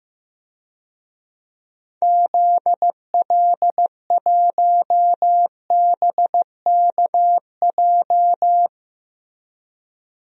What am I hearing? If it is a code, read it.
Morse code, heard as a keyed tone, about 15 wpm, 700 Hz. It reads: ZL1BKJ
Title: DXCC-100 Call Signs - 6 Characters - Repeated 2x - 15wpm